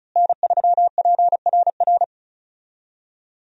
Morse code, heard as a keyed tone, about 36 wpm, 700 Hz. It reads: N3PRR
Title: USA Call Signs - 5 Characters - Repeated 3x - 35wpm